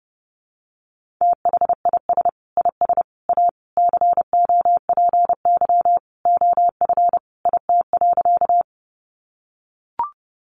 Morse code, heard as a keyed tone, about 28 wpm, 700 Hz. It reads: THIS IS A COPY OF IT.
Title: Sentences from Top 1400 Words 30wpm